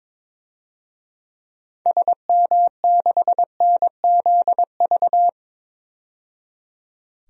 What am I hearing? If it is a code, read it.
Morse code, heard as a keyed tone, about 22 wpm, 700 Hz. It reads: SM6NZV
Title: DXCC-100 Call Signs - 6 Characters - 22wpm